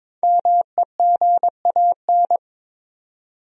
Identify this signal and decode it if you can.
Morse code, heard as a keyed tone, about 21 wpm, 700 Hz. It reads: MEGAN
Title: USA Common Names - 22wpm